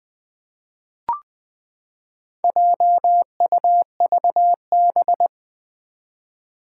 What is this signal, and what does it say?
Morse code, heard as a keyed tone, about 20 wpm, 700 Hz. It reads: JUVB
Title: Four Challenging Letter-Number Combinations 20wpm